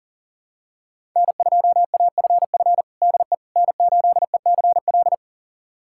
Morse code, heard as a keyed tone, about 40 wpm, 700 Hz. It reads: N2AFF DE N8ECL
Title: USA Callsign DE Callsign - 5 Characters - 40wpm